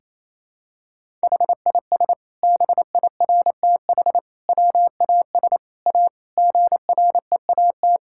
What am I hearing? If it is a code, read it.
Morse code, heard as a keyed tone, about 28 wpm, 700 Hz. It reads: HIS BIRTH WAS A GREAT
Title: Sentences from Top 1000 Words - Review Part 5 - 28wpm